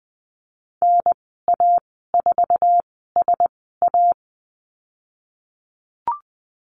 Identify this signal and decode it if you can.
Morse code, heard as a keyed tone, about 20 wpm, spaced as for 15 wpm, 700 Hz. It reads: NA4SA
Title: 076 Lesson 13g - Letter L - Call Signs 20wpm